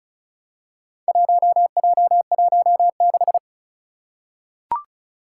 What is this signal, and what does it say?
Morse code, heard as a keyed tone, about 35 wpm, 690 Hz. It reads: 1J16
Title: Four Challenging Letter-Number Combinations 35wpm